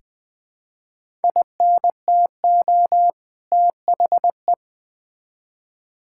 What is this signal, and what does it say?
Morse code, heard as a keyed tone, about 20 wpm, 700 Hz. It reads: INTO THE
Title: Build A Phrase - 2-Word Phrases - 20wpm